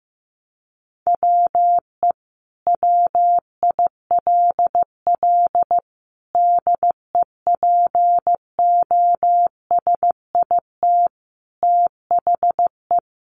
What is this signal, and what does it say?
Morse code, heard as a keyed tone, about 14 wpm, 700 Hz. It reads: WE WILL DEPOSIT THE
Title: Sentences from Top 1300 Words 15wpm